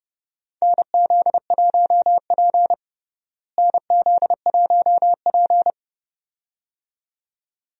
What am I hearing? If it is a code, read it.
Morse code, heard as a keyed tone, about 32 wpm, 700 Hz. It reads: NZ1P NZ1P
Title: USA Call Signs - 4 Characters - Repeated 2x - 30wpm